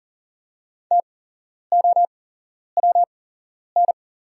Morse code, heard as a keyed tone, about 39 wpm, spaced as for 11 wpm, 690 Hz. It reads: TOWN
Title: 8x Character Spacing - Top 500 Words - 40wpm